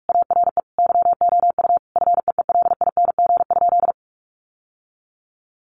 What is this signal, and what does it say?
Morse code, heard as a keyed tone, about 44 wpm, 700 Hz. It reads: ARE YOU FEELING?
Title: Sentences from Top 1200 Words 45wpm